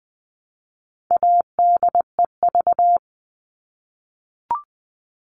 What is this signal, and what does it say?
Morse code, heard as a keyed tone, about 20 wpm, 700 Hz. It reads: ADEV
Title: All Cut Numbers - 4-Digit 20wpm